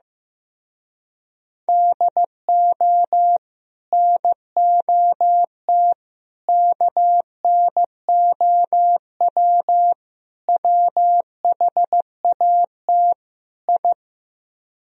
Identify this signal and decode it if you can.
Morse code, heard as a keyed tone, about 16 wpm, 700 Hz. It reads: DO NOT KNOW WHAT I
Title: Speed-Racing - Sets of 5 Words 15wpm